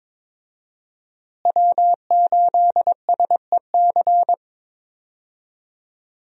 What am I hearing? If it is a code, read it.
Morse code, heard as a keyed tone, about 22 wpm, 700 Hz. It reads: W8SEC
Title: USA Call Signs - 5 Characters - Repeated 2x - 22wpm